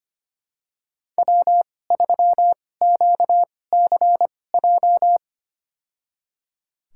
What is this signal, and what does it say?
Morse code, heard as a keyed tone, about 25 wpm, spaced as for 18 wpm, 700 Hz. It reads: W3QCJ
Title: USA Call Signs - 5 Characters - 2x Character Spacing - 25wpm